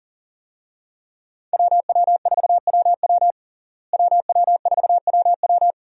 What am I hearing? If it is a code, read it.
Morse code, heard as a keyed tone, about 40 wpm, 700 Hz. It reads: WW4WW WW4WW
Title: USA Call Signs - 5 Characters - Repeated 2x - 40wpm